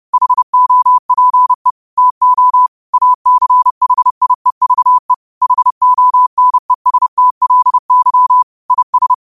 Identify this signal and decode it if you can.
Morse code, heard as a keyed tone, about 30 wpm, 1000 Hz. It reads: HOPE TO ACHIEVE HONESTLY IS
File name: qotdfiles / 30wpm